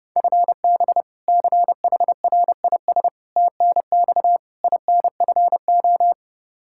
Morse code, heard as a keyed tone, about 32 wpm, 700 Hz. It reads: FB CHRIS TNX INFO
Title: Practice QSO - Exchange 3 - More Info - 30wpm